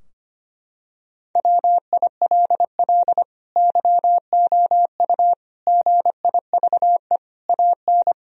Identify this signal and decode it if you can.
Morse code, heard as a keyed tone, about 25 wpm, 700 Hz. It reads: WILL YOU GIVE AN
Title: Sentences from Top 1000 Words - Review Part 3 - 25wpm